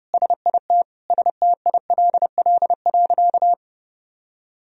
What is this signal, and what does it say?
Morse code, heard as a keyed tone, about 30 wpm, 700 Hz. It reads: SIT STILL.
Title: Sentences from Top 500 Words and Abbreviations 30wpm